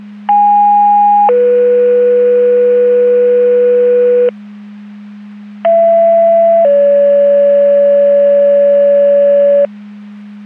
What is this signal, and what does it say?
Two-tone page, heard as a one-second tone, then a three-second tone, about 820 then 490 Hz, about 670 then 570 Hz. It is heard from 0.3 to 9.6 s.